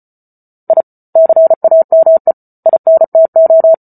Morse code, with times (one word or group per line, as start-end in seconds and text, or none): 0.7-0.8 I
1.2-2.3 CAME
2.7-3.7 INTO